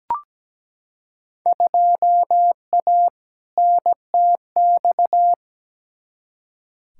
Morse code, heard as a keyed tone, about 17 wpm, 700 Hz. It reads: 2A NTX
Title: Field Day Exchanges - 17wpm